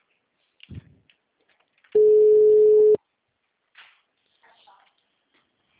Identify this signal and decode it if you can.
Ringback tone (Europe, 425 Hz).
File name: out- -2005- - - .40